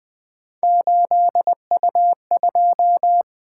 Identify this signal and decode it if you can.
Morse code, heard as a keyed tone, about 24 wpm, 700 Hz. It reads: 8U2
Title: Speed-Racing - Three Letter-Number Combinations - 20wpm